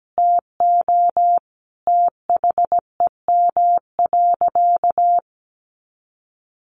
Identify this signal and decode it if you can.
Morse code, heard as a keyed tone, about 17 wpm, 700 Hz. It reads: TO THEM.
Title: Sentences from Top 200 Words 17wpm